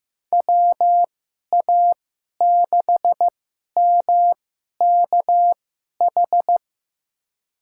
Morse code, heard as a keyed tone, about 15 wpm, spaced as for 11 wpm, 700 Hz. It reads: WA6MKH
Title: USA Call Signs - 6 Characters - 2x Character Spacing - 15wpm